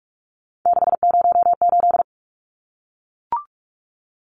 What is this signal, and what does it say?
Morse code, heard as a keyed tone, about 44 wpm, 700 Hz. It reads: 608
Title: Numbers - 3-Digit 45wpm